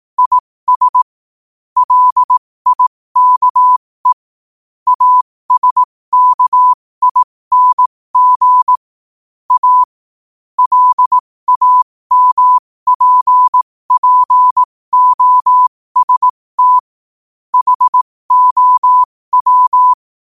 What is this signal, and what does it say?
Morse code, heard as a keyed tone, about 18 wpm, spaced as for 15 wpm, 1000 Hz. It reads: IS LIKE ASKING A LAMPPOST HOW
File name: qotdfiles / 15wpm